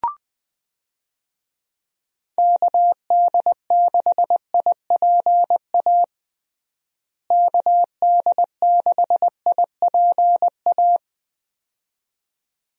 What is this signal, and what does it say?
Morse code, heard as a keyed tone, about 20 wpm, 700 Hz. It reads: KD6IPA KD6IPA
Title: USA Call Signs - 6 Characters - Repeated 2x - 20wpm